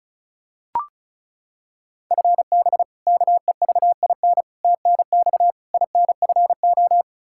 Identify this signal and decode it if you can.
Morse code, heard as a keyed tone, about 36 wpm, 700 Hz. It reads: FB KEVIN TNX INFO
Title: Practice QSO - Exchange 3 - More Info - 35wpm